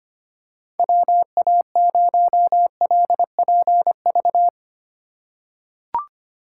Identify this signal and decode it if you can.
Morse code, heard as a keyed tone, about 25 wpm, 700 Hz. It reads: WA0LPV
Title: K1USN SST QSO Practice - 25wpm